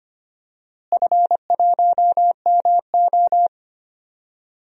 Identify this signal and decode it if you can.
Morse code, heard as a keyed tone, about 25 wpm, 700 Hz. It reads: F1MO
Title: DXCC-100 Call Signs - 4 Characters - Repeated 3x - 25wpm